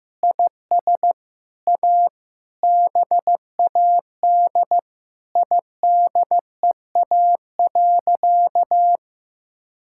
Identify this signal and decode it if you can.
Morse code, heard as a keyed tone, about 15 wpm, 700 Hz. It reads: IS A BAD IDEA.
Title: Sentences from Top 1000 Words - Review Part 5 - 15wpm